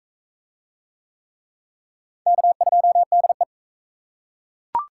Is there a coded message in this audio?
Yes: Morse code from 2.3 to 3.4 s, heard as a keyed tone, about 37 wpm, 700 Hz. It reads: K2DE